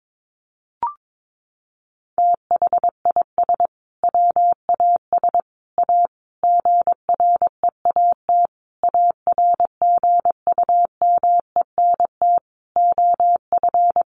Morse code, heard as a keyed tone, about 22 wpm, 700 Hz. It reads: THIS WAS A GREAT ARGUMENT OF
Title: Sentences from Top 1400 Words 22wpm